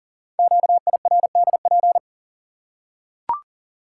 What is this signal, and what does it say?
Morse code, heard as a keyed tone, about 43 wpm, 700 Hz. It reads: QIRDP
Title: Code Groups - 5 Letter:Numbers - 40wpm